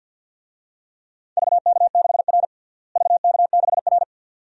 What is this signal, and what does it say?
Morse code, heard as a keyed tone, about 48 wpm, 700 Hz. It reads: VK6R VK6R
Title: DXCC-100 Call Signs - 4 Characters - Repeated 2x - 50wpm